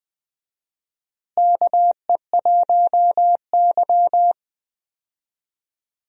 Morse code, heard as a keyed tone, about 20 wpm, 700 Hz. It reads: KE1Y
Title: USA Call Signs - 4 Characters - Repeated 2x - 20wpm